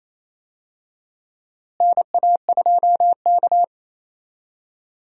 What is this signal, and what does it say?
Morse code, heard as a keyed tone, about 28 wpm, 700 Hz. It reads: NA2K
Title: USA Call Signs - 4 Characters - Repeated 2x - 28wpm